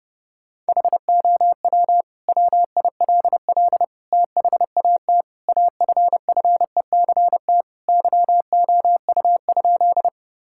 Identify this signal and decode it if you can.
Morse code, heard as a keyed tone, about 30 wpm, 700 Hz. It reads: HOW WILL THAT AFFECT YOU?
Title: Sentences from Top 1000 Words - Review Part 6 - 30wpm